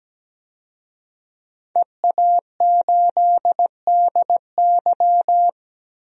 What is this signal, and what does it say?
Morse code, heard as a keyed tone, about 17 wpm, 700 Hz. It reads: EA8DY
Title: DXCC-100 Call Signs - 5 Characters - 17wpm